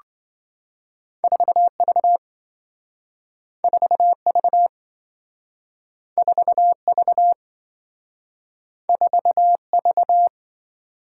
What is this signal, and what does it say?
Morse code, heard as a keyed tone, about 26 wpm, 700 Hz. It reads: 4V 4V 4V 4V